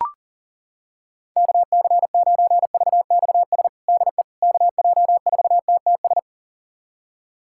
Morse code, heard as a keyed tone, about 40 wpm, 700 Hz. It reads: KC9VXS DE KJ4TTS
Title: USA Callsign DE Callsign - 6 Characters - 40wpm